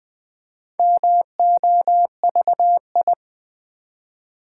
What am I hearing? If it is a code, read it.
Morse code, heard as a keyed tone, about 20 wpm, 700 Hz. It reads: MOVI
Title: Build A Word - 1 Letter At A Time - 5-Letter Words - 20wpm